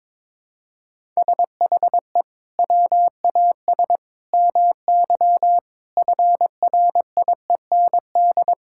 Morse code, heard as a keyed tone, about 22 wpm, 700 Hz. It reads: SHE WAS MY FRIEND
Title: Sentences from Top 1000 Words - Review Part 5 - 22wpm